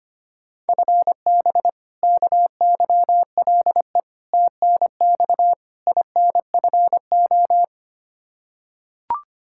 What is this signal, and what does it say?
Morse code, heard as a keyed tone, about 25 wpm, 700 Hz. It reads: FB KYLE TNX INFO
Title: Practice QSO - Exchange 3 - More Info - 25wpm